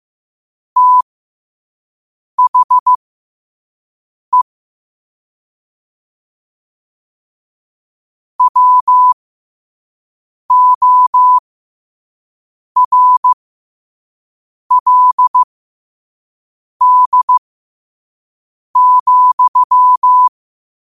Morse code, heard as a keyed tone, about 15 wpm, spaced as for 5 wpm, 1000 Hz. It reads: THE WORLD,
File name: qotdfiles / 5wpm